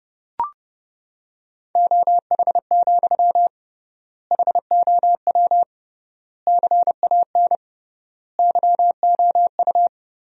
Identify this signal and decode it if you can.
Morse code, heard as a keyed tone, about 30 wpm, 700 Hz. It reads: OH, HOW CAN YOU
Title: 3x Word-Spacing - Sentences from Top 500 Words - 30wpm